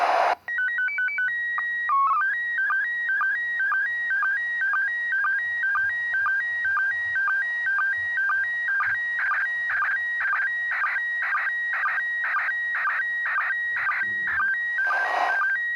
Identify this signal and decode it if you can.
SSTV (slow-scan television), PD 120.
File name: ARISS-Expedition-64-UTC-07-11